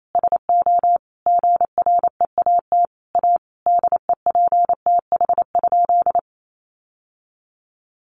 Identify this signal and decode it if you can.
Morse code, heard as a keyed tone, about 28 wpm, 700 Hz. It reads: SO GREAT A DEPTH?
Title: Sentences from Top 1200 Words 28wpm